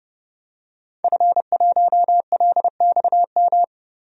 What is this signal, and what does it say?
Morse code, heard as a keyed tone, about 32 wpm, 700 Hz. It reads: F1LXM
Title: DXCC-100 Call Signs - 5 Characters - Repeated 2x - 30wpm